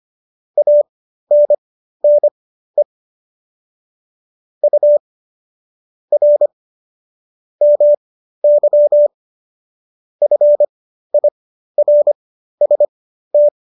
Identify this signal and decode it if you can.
Morse code, heard as a keyed tone, about 25 wpm, spaced as for 13 wpm, 590 Hz. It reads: ANNE U R MY FIRST